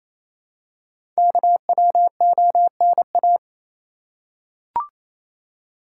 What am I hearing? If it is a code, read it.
Morse code, heard as a keyed tone, about 29 wpm, 700 Hz. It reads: KWONA
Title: Code Groups - 5 Letters - 28wpm